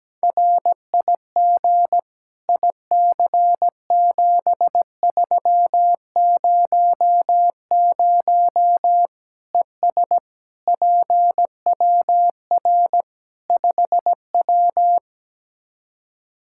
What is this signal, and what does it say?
Morse code, heard as a keyed tone, about 17 wpm, 700 Hz. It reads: RIG IC7300 ES PWR 5W
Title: Practice QSO - Exchange 2 - Info - 17wpm